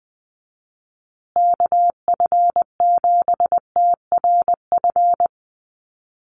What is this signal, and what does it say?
Morse code, heard as a keyed tone, about 20 wpm, 700 Hz. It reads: KF7TRF
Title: USA Call Signs Repeated 2x 20wpm